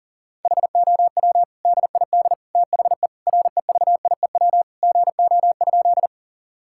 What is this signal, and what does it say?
Morse code, heard as a keyed tone, about 40 wpm, 690 Hz. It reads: HOW DID THE REVIEW GO?
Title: Sentences from Top 1000 Words - Review Part 6 - 40wpm